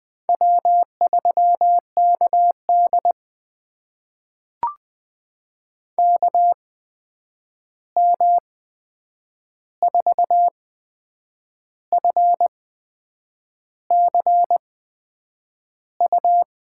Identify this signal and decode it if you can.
Morse code, heard as a keyed tone, about 20 wpm, spaced as for 8 wpm, 700 Hz. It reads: W3KD KM4FCU
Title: USA Call Signs - 8x Character Spacing - 20wpm